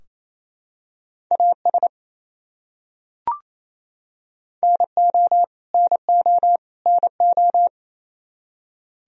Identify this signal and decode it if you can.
Morse code, heard as a keyed tone, about 28 wpm, 700 Hz. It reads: AS NO NO NO
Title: Top 100 Words Repeated 28wpm